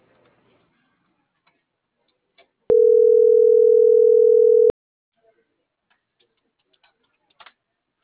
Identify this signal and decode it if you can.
Ringback tone (North America), 440 + 480 Hz.